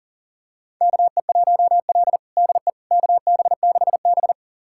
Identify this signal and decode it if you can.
Morse code, heard as a keyed tone, about 40 wpm, 700 Hz. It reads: KE1L DE KB6B
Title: USA Callsign DE Callsign - 4 Characters - 40wpm